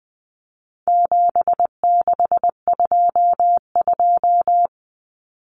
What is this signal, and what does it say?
Morse code, heard as a keyed tone, about 20 wpm, 700 Hz. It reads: 7622
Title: Common Cut Numbers - 4-Digit 20wpm